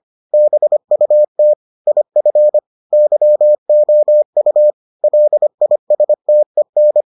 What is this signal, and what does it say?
Morse code, heard as a keyed tone, about 25 wpm, 600 Hz. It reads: BUT IF YOU LISTEN